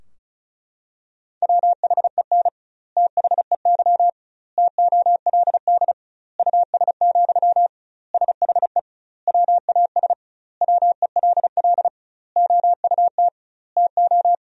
Morse code, heard as a keyed tone, about 36 wpm, 700 Hz. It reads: WHEN THEY TOLD US, SHE WAS WELL OUT TO
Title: 2x Word-Spacing - Sentences from Top 400 Words - 35wpm